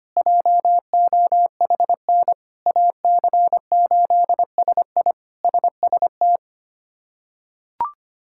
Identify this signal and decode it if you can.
Morse code, heard as a keyed tone, about 25 wpm, 700 Hz. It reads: JOHN AC8SI SST
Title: K1USN SST QSO Practice - 25wpm